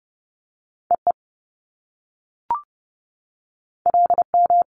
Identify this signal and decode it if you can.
Morse code, heard as a keyed tone, about 30 wpm, 700 Hz. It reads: EE LM
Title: Top Two Letter Combinations in Words 30wpm